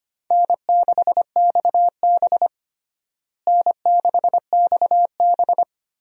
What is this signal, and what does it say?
Morse code, heard as a keyed tone, about 25 wpm, 700 Hz. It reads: N6XB N6XB
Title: USA Call Signs - 4 Characters - Repeated 2x - 25wpm